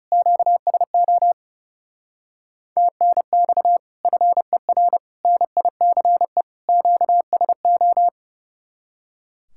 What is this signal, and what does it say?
Morse code, heard as a keyed tone, about 36 wpm, 700 Hz. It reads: QSO TNX FER NICE QSO
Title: Speed-Racing Common QSO Phrases - 30wpm